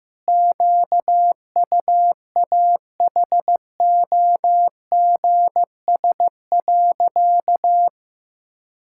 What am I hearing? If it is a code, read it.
Morse code, heard as a keyed tone, about 15 wpm, 700 Hz. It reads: QUAHOGS.
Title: Pangram Challenge v2 - 15wpm